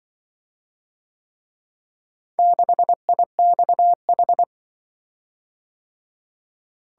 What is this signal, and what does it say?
Morse code, heard as a keyed tone, about 23 wpm, 700 Hz. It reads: 6IXH
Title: Speed-Racing - Four Letter-Number Combinations - 20wpm